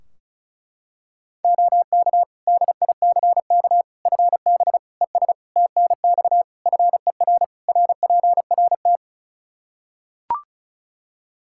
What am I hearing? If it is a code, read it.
Morse code, heard as a keyed tone, about 36 wpm, 700 Hz. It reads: OK DICK FB ES TNX FER RPRT
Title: Practice QSO - Exchange 2 - Info - 35wpm